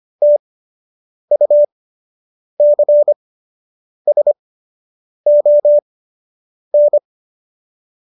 Morse code, heard as a keyed tone, about 25 wpm, spaced as for 8 wpm, 590 Hz. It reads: TUCSON
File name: qso006 08